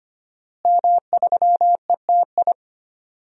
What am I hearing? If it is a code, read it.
Morse code, heard as a keyed tone, about 25 wpm, 700 Hz. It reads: M3ETI
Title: DXCC-100 Call Signs - 5 Characters - Repeated 3x - 25wpm